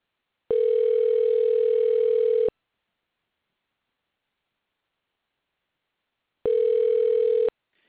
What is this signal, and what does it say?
Ringback tone (North America), 440 + 480 Hz.